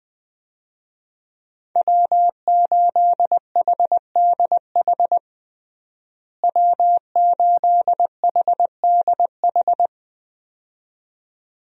Morse code, heard as a keyed tone, about 21 wpm, 700 Hz. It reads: W8HDH W8HDH
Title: USA Call Signs - 5 Characters - Repeated 2x - 20wpm